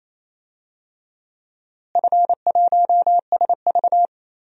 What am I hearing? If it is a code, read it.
Morse code, heard as a keyed tone, about 28 wpm, 700 Hz. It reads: F1SV
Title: DXCC-100 Call Signs - 4 Characters - Repeated 3x - 28wpm